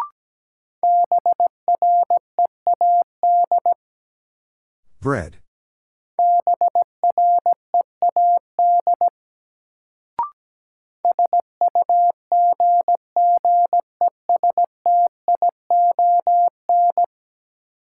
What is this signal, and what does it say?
Morse code, heard as a keyed tone, about 17 wpm, 700 Hz. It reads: BREAD BREAD SUGGESTION
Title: Sentences from Top 1600 Words - New Word Drill - 17wpm